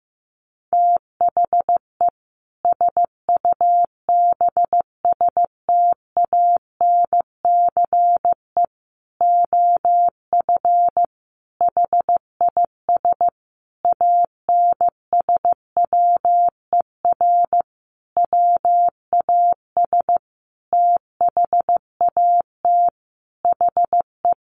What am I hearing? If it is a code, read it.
Morse code, heard as a keyed tone, about 14 wpm, 700 Hz. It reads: THE SUBSTANCE OF HIS ANSWER WAS THAT HE
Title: Sentences from Top 1500 Words 15wpm